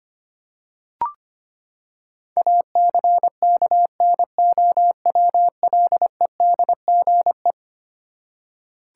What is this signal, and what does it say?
Morse code, heard as a keyed tone, about 25 wpm, 700 Hz. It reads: ACKNOWLEDGE
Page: morse-code-ninja-practice-25wpm